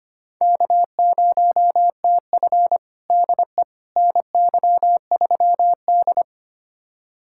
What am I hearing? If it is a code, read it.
Morse code, heard as a keyed tone, about 25 wpm, 700 Hz. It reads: K0TF DE NY3D
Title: USA Callsign DE Callsign - 4 Characters - 25wpm